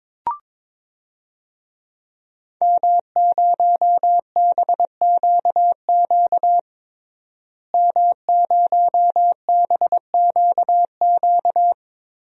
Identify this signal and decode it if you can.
Morse code, heard as a keyed tone, about 22 wpm, 700 Hz. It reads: M0BQQ M0BQQ
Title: DXCC-100 Call Signs - 5 Characters - Repeated 2x - 22wpm